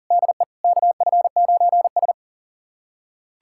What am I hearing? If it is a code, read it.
Morse code, heard as a keyed tone, about 40 wpm, 700 Hz. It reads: DE KF9S
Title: USA Callsign DE Callsign - 4 Characters - 40wpm